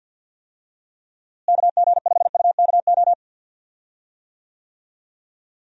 Morse code, heard as a keyed tone, about 47 wpm, 700 Hz. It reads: KG5UKO
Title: USA Call Signs - 6 Characters - Repeated 3x - 50wpm